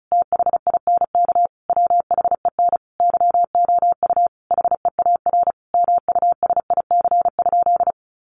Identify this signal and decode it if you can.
Morse code, heard as a keyed tone, about 34 wpm, 700 Hz. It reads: THINK WHEN YOU HEAR MUSIC?
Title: Sentences from Top 300 Words 35wpm